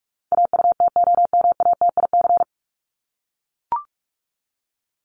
Morse code, heard as a keyed tone, about 44 wpm, 700 Hz. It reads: AUTOMATIC
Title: Top 1751-2000 Words 45wpm